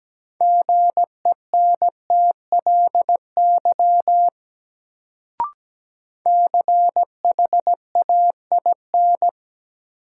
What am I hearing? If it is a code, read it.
Morse code, heard as a keyed tone, about 17 wpm, 700 Hz. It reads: GENTLY CHAIN
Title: Sentences from Top 1500 Words - New Word Drill - 17wpm